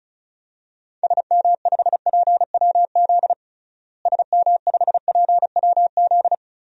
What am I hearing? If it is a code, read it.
Morse code, heard as a keyed tone, about 36 wpm, 700 Hz. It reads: SM5PWZ SM5PWZ
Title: DXCC-100 Call Signs - 6 Characters - Repeated 3x - 35wpm